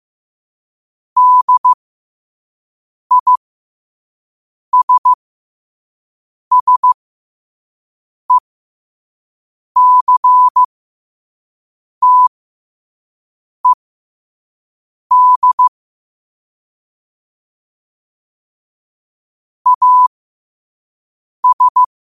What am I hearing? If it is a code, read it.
Morse code, heard as a keyed tone, about 15 wpm, spaced as for 5 wpm, 1000 Hz. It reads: DISSECTED AS
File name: qotdfiles / 5wpm